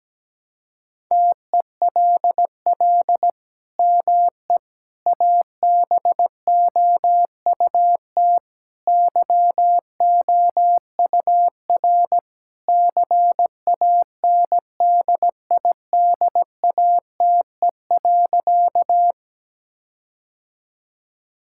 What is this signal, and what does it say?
Morse code, heard as a keyed tone, about 17 wpm, 700 Hz. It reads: TELL ME ABOUT YOUR CANDIDATE.
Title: Sentences from Top 1600 Words 17wpm